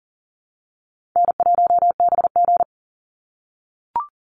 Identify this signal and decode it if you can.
Morse code, heard as a keyed tone, about 40 wpm, 700 Hz. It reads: N1BG
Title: Easy Call Signs - 40wpm